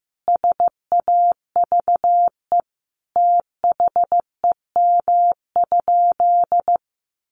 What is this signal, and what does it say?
Morse code, heard as a keyed tone, about 15 wpm, 700 Hz. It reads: SAVE THEM?
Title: Sentences from Top 1100 Words 15wpm